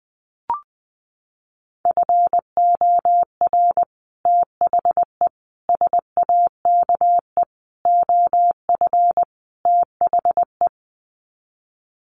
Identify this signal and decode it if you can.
Morse code, heard as a keyed tone, about 20 wpm, 700 Hz. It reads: FOR THE SAKE OF THE
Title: Sets of 5 Words Encore 20wpm